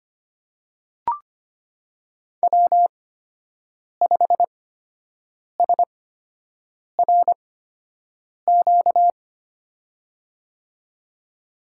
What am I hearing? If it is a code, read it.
Morse code, heard as a keyed tone, about 25 wpm, spaced as for 7 wpm, 700 Hz. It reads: W5SRQ
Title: USA Call Signs - 8x Character Spacing - 25wpm